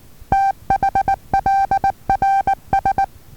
Morse code, heard as a keyed tone, about 19 wpm, 780 Hz. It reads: THLRS
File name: Taviro HA6VA B